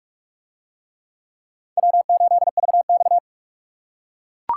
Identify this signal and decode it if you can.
Morse code, heard as a keyed tone, about 42 wpm, 700 Hz. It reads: W8VX